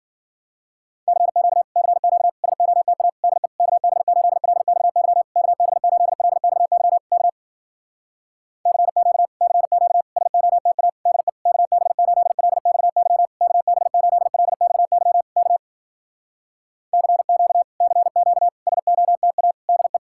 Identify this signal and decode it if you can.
Morse code, heard as a keyed tone, about 53 wpm, 700 Hz. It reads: CQ CQ SOTA DE KB8LXY KB8LXY K CQ CQ SOTA DE KB8LXY KB8LXY K CQ CQ SOTA DE